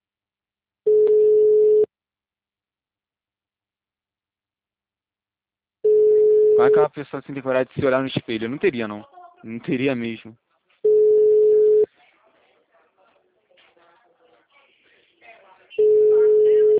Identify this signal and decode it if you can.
Ringback tone (Europe, 425 Hz), with speech.
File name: out- -2006